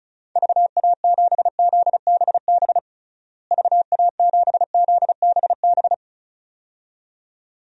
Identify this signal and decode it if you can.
Morse code, heard as a keyed tone, about 36 wpm, 700 Hz. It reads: VA7ZBB VA7ZBB
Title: DXCC-100 Call Signs - 6 Characters - Repeated 2x - 35wpm